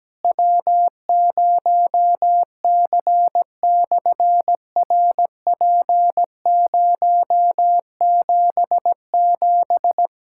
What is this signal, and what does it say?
Morse code, heard as a keyed tone, about 17 wpm, 700 Hz. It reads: W0C/RP077
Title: SOTA QSO Practice 17wpm